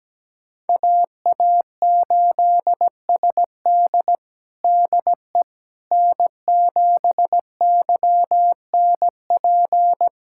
Morse code, heard as a keyed tone, about 17 wpm, 700 Hz. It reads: AA8SD DE N7YNP
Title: USA Callsign DE Callsign - 5 Characters - 17wpm